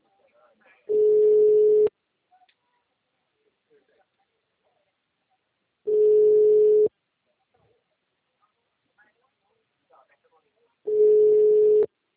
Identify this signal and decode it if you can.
Ringback tone (Europe, 425 Hz), with speech.